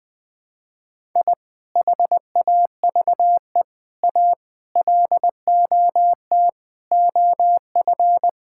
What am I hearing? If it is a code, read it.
Morse code, heard as a keyed tone, about 21 wpm, 700 Hz. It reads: I HAVE A LOT OF
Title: Sentences with 5-Word Phrases - 20wpm